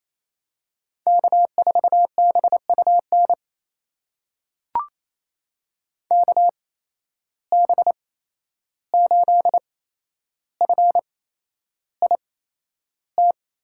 Morse code, heard as a keyed tone, about 28 wpm, spaced as for 11 wpm, 700 Hz. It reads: K4BUN KB8FIT
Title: USA Call Signs - 8x Character Spacing - 28wpm